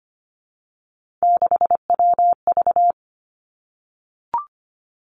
Morse code, heard as a keyed tone, about 25 wpm, 700 Hz. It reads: 6WV
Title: Three Challenging Letter-Number Combinations 25wpm